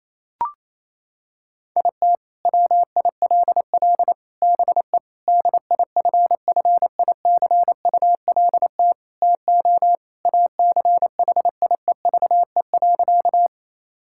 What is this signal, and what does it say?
Morse code, heard as a keyed tone, about 28 wpm, 690 Hz. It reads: IT WILL BE DIFFICULT TO ACHIEVE.
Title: Sentences from Top 1000 Words - Review Part 6 - 28wpm